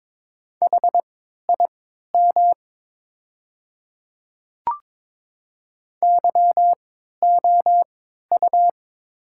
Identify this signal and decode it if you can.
Morse code, heard as a keyed tone, about 22 wpm, spaced as for 13 wpm, 700 Hz. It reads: HIM YOU
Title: 3x Character Spacing - Top 200 Words - 22wpm